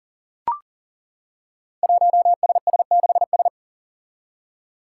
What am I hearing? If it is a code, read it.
Morse code, heard as a keyed tone, about 39 wpm, 700 Hz. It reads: 1SS6S
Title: Five Challenging Letter-Number Combinations 40wpm